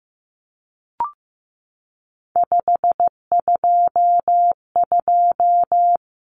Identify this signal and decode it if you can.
Morse code, heard as a keyed tone, about 15 wpm, 700 Hz. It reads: 522
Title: Numbers - 3-Digit 15wpm